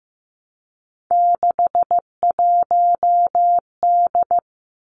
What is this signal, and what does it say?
Morse code, heard as a keyed tone, about 15 wpm, 700 Hz. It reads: 61D
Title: Three Challenging Letter-Number Combinations 15wpm